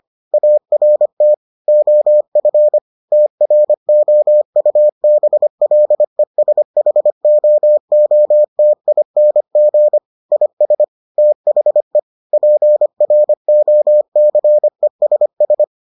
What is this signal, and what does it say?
Morse code, heard as a keyed tone, about 25 wpm, 600 Hz. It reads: ART OF TROUBLESHOOTING IS THE PROCESS